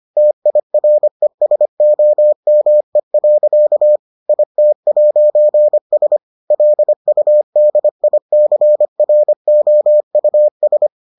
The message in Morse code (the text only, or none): TIRESOME. IT'S LUDICROUS